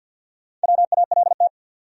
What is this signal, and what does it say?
Morse code, heard as a keyed tone, about 50 wpm, 700 Hz. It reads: WALT
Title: USA Common Names - Max 4 Letters - 50wpm